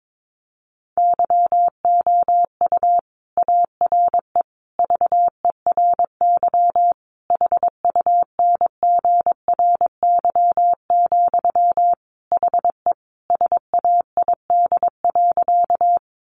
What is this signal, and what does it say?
Morse code, heard as a keyed tone, about 22 wpm, 700 Hz. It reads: YOU ARE VERY HUNGRY, HE SAID.
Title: Sentences from Top 1400 Words 22wpm